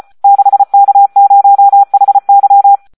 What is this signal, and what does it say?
Morse code, heard as a keyed tone, about 34 wpm, 800 Hz. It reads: 6K0HY
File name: Audio practica CW / SD 1 34